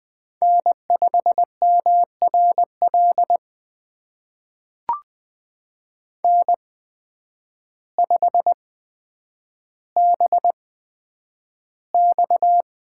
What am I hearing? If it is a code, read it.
Morse code, heard as a keyed tone, about 20 wpm, spaced as for 9 wpm, 700 Hz. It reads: N5MRL N5BX
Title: USA Call Signs - 8x Character Spacing - 20wpm